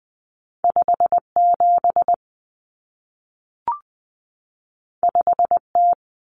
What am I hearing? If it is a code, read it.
Morse code, heard as a keyed tone, about 20 wpm, 700 Hz. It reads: 57 5T
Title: Common Cut Numbers - 2-Digit 20wpm